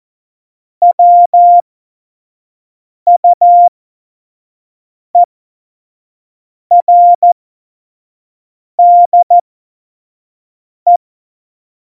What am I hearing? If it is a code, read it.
Morse code, heard as a keyed tone, about 14 wpm, spaced as for 5 wpm, 700 Hz. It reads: WUERDE
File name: Morsekurs / Witze / Witz 07. Januar 2020, 25 F70